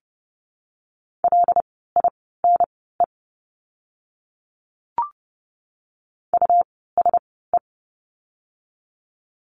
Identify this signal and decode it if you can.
Morse code, heard as a keyed tone, about 30 wpm, spaced as for 17 wpm, 700 Hz. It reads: LINE USE
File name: course / 30:10wpm / 115 Lesson 19f - Letter w - Words 30wpm